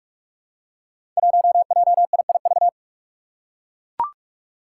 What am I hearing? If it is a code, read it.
Morse code, heard as a keyed tone, about 44 wpm, 700 Hz. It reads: 1JIIV
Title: Five Challenging Letter-Number Combinations 45wpm